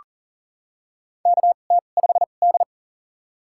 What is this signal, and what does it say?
Morse code, heard as a keyed tone, about 40 wpm, spaced as for 29 wpm, 700 Hz. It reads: KT5D
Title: USA Call Signs - 4 Characters - 2x Character Spacing - 40wpm